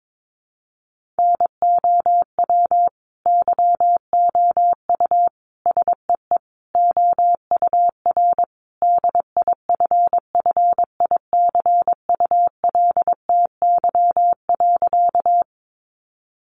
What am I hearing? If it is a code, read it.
Morse code, heard as a keyed tone, about 22 wpm, 700 Hz. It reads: NOW YOU SEE OUR DIFFICULTY.
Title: Sentences from Top 1500 Words 22wpm